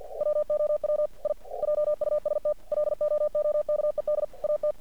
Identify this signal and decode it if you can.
Morse code, heard as a keyed tone, about 50 wpm, 600 Hz. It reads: WOW I JUST LOOKED AT
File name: archived / CRSnet